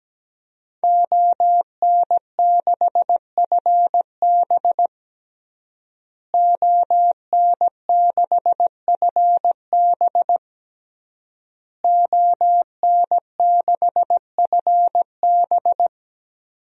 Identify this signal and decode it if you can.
Morse code, heard as a keyed tone, about 17 wpm, 700 Hz. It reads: ON6FB ON6FB ON6FB
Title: DXCC-100 Call Signs - 5 Characters - Repeated 3x - 17wpm